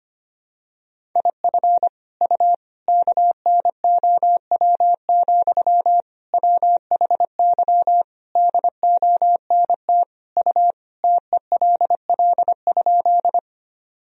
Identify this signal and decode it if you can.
Morse code, heard as a keyed tone, about 25 wpm, 700 Hz. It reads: IF U KNOW, WHY DONT U TELL?
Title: Sentences from Top 300 Words and Abbreviations 25wpm